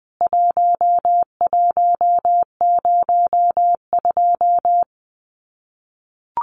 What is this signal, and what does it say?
Morse code, heard as a keyed tone, about 20 wpm, 700 Hz. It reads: 1102
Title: CWT Exchange 20wpm